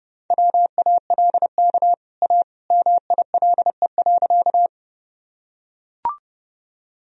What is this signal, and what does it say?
Morse code, heard as a keyed tone, about 30 wpm, 700 Hz. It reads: WALK A MILE.
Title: Sentences from Top 1000 Words - Review Part 1 - 30wpm